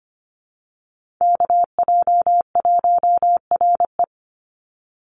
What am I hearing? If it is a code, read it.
Morse code, heard as a keyed tone, about 25 wpm, 700 Hz. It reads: KJ1RE
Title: Call Signs Repeated 3x - 25wpm